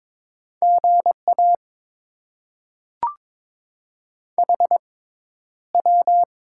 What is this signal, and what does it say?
Morse code, heard as a keyed tone, about 22 wpm, spaced as for 11 wpm, 700 Hz. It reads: GA HW
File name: Top 20 QSO Elements - 6x Character Spacing - 22wpm